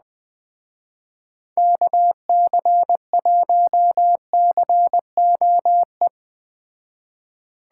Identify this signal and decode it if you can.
Morse code, heard as a keyed tone, about 20 wpm, 700 Hz. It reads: KC1COE